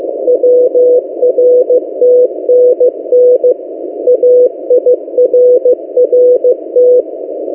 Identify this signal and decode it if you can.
Morse code, heard as a keyed tone, about 15 wpm, 500 Hz. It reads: WRTNN AIRRT
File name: M08a- -1355utc-8096khz